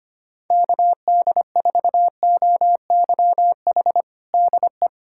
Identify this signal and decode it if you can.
Morse code, heard as a keyed tone, about 25 wpm, 700 Hz. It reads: KD4OYH DE
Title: USA Callsign DE Callsign - 6 Characters - 25wpm